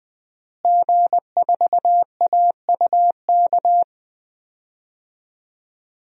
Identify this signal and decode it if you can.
Morse code, heard as a keyed tone, about 20 wpm, 700 Hz. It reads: G4AUK
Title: DXCC-100 Call Signs - 5 Characters - Repeated 3x - 20wpm